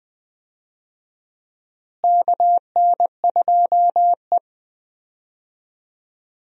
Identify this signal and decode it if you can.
Morse code, heard as a keyed tone, about 21 wpm, 700 Hz. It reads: KN2E
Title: USA Call Signs - 4 Characters - Repeated 3x - 20wpm